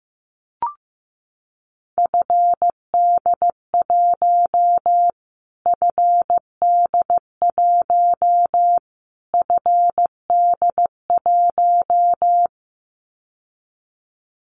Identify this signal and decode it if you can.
Morse code, heard as a keyed tone, about 15 wpm, 700 Hz. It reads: FD1 FD1 FD1
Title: Callsign Prefixes for Top 25 DXCC - Repeated 3x - 15wpm